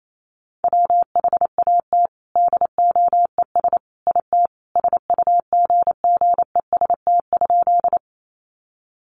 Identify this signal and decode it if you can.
Morse code, heard as a keyed tone, about 28 wpm, 700 Hz. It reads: WHAT DOES IT SUGGEST?
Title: Sentences from Top 1300 Words 28wpm